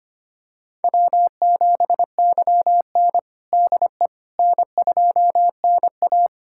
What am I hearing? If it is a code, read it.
Morse code, heard as a keyed tone, about 25 wpm, 700 Hz. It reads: W7YN DE N2NA
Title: USA Callsign DE Callsign - 4 Characters - 25wpm